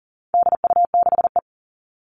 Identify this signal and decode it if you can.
Morse code, heard as a keyed tone, about 40 wpm, 700 Hz. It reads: DU6E
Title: All Cut Numbers - 4-Digit 40wpm